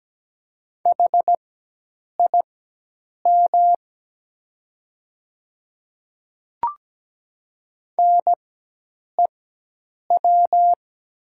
Morse code, heard as a keyed tone, about 17 wpm, spaced as for 8 wpm, 700 Hz. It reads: HIM NEW
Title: 4x Character Spacing - Top 200 Words - 17wpm